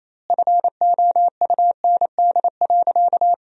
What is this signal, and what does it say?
Morse code, heard as a keyed tone, about 28 wpm, 700 Hz. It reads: FOUND.
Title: 2x Word-Spacing - Sentences from Top 300 Words - 28wpm